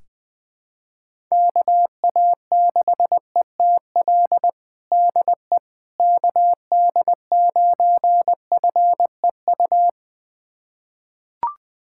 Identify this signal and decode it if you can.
Morse code, heard as a keyed tone, about 20 wpm, 700 Hz. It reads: KA6ETL DE KD9FEU
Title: USA Callsign DE Callsign - 6 Characters - 20wpm